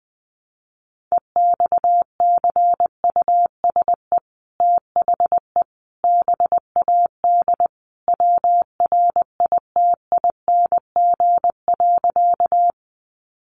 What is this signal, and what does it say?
Morse code, heard as a keyed tone, about 20 wpm, 700 Hz. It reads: EXCUSE THE BAD WRITING.
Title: Sentences from Top 800 Words 20wpm